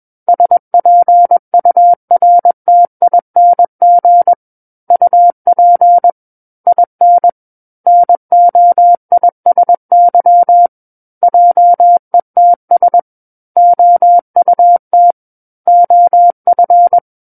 Morse code, heard as a keyed tone, about 21 wpm, 700 Hz. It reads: SPURTING UP IN NOISY JETS OUT OF
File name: Book One / 12-Chapter 12 - 21 WPM